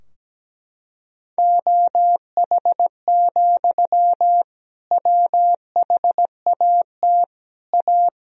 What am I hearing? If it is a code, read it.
Morse code, heard as a keyed tone, about 17 wpm, 700 Hz. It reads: OH, WHAT A
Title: Sentences from Top 1000 Words - Review Part 2 - 17wpm